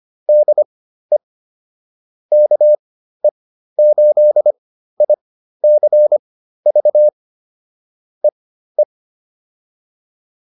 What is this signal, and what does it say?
Morse code, heard as a keyed tone, about 25 wpm, spaced as for 13 wpm, 600 Hz. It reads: DE KE8ICV EE